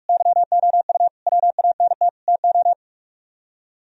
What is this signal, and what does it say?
Morse code, heard as a keyed tone, about 57 wpm, 700 Hz. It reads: YOU WANT TO